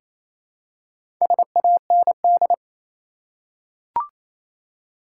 Morse code, heard as a keyed tone, about 28 wpm, 700 Hz. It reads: SAND
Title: Four Letter Words - 28wpm